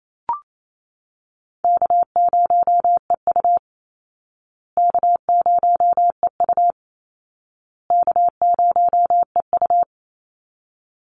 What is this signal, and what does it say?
Morse code, heard as a keyed tone, about 28 wpm, 700 Hz. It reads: K0EU K0EU K0EU
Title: Call Signs Repeated 2x - 28wpm